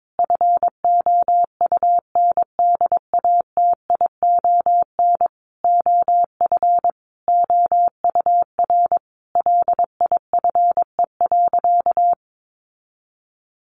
Morse code, heard as a keyed tone, about 22 wpm, 700 Hz. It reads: FOUNDATION OF OUR LIFE.
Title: Sentences from Top 1200 Words 22wpm